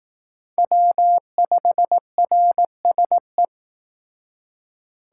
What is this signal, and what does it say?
Morse code, heard as a keyed tone, about 18 wpm, 700 Hz. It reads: W5RSE
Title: Speed Racing - USA 1x3 Call Signs 15wpm